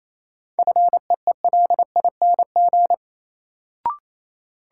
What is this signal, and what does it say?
Morse code, heard as a keyed tone, about 28 wpm, 700 Hz. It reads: FEELING
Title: Sentences from Top 1000 Words - Review Part 3 - New Word Drill - 28wpm